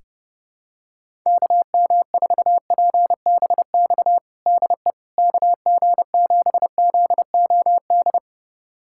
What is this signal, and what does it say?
Morse code, heard as a keyed tone, about 30 wpm, 700 Hz. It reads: KM4PBX DE KG7ZOD
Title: USA Callsign DE Callsign - 6 Characters - 30wpm